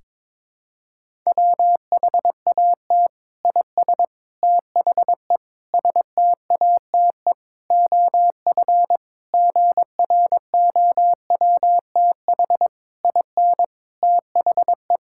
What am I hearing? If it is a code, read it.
Morse code, heard as a keyed tone, about 22 wpm, 700 Hz. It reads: WHAT IS THE STATE OF GROWTH IN THE
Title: Sentences from Top 1000 Words - Review Part 5 - 22wpm